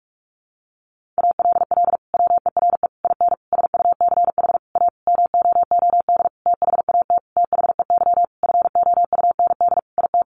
Morse code, heard as a keyed tone, about 44 wpm, 700 Hz. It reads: ALL WERE IN SUCH A MOOD THAT THEY FOUND IT
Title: Sentences from Top 1100 Words 45wpm